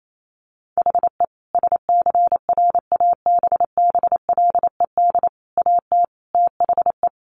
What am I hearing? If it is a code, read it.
Morse code, heard as a keyed tone, about 28 wpm, 700 Hz. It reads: HE SCRABBLED AT THE
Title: A Sound Of Thunder 28wpm